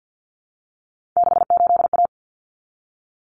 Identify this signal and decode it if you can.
Morse code, heard as a keyed tone, about 50 wpm, 700 Hz. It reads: 68A
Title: Common Cut Numbers - 3-Digit 50wpm